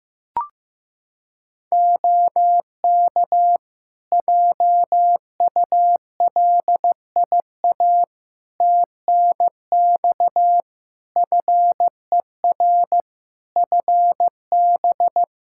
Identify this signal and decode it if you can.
Morse code, heard as a keyed tone, about 15 wpm, 700 Hz. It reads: OK JULIA TNX FER FB
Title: Practice QSO - Exchange 4 - Ending - 15wpm